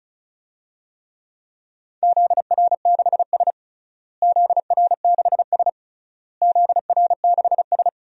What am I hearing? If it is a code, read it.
Morse code, heard as a keyed tone, about 34 wpm, 700 Hz. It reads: ZR6S ZR6S ZR6S
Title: DXCC-100 Call Signs - 4 Characters - Repeated 3x - 35wpm